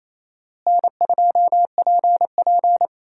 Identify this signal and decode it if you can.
Morse code, heard as a keyed tone, about 28 wpm, 700 Hz. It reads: N2PP
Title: USA Call Signs - 4 Characters - Repeated 2x - 28wpm